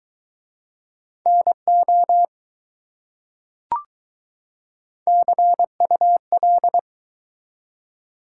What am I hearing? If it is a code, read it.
Morse code, heard as a keyed tone, about 22 wpm, 700 Hz. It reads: NO CUL